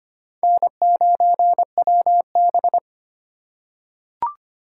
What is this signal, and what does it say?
Morse code, heard as a keyed tone, about 25 wpm, 700 Hz. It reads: N9WB
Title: USA Call Signs - 4 Characters - 3x Character Spacing - 25wpm